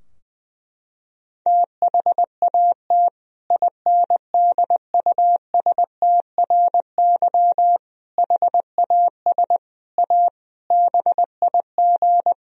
Morse code, heard as a keyed tone, about 20 wpm, 700 Hz. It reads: THAT INDUSTRY HAS A BIG
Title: Sentences from Top 1000 Words - Review Part 4 - 20wpm